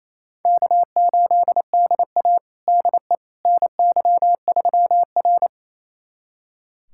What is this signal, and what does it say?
Morse code, heard as a keyed tone, about 28 wpm, 700 Hz. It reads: K8DA DE NY3R
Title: USA Callsign DE Callsign - 4 Characters - 28wpm